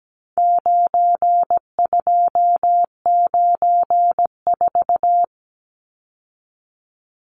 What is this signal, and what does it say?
Morse code, heard as a keyed tone, about 17 wpm, 700 Hz. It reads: 9294
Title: Numbers - 4-Digit 17wpm